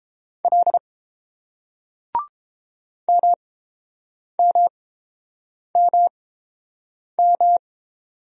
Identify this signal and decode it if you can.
Morse code, heard as a keyed tone, about 30 wpm, spaced as for 7 wpm, 690 Hz. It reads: L MMMM